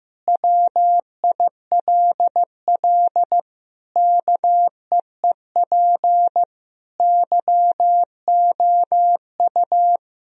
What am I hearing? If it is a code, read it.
Morse code, heard as a keyed tone, about 15 wpm, 700 Hz. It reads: WILL KEEP YOU
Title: Sentences from Top 1000 Words - Review Part 3 - 15wpm